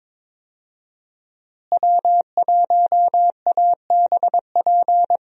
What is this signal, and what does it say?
Morse code, heard as a keyed tone, about 22 wpm, 700 Hz. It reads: W1ABP
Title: USA Call Signs - 5 Characters - Repeated 2x - 22wpm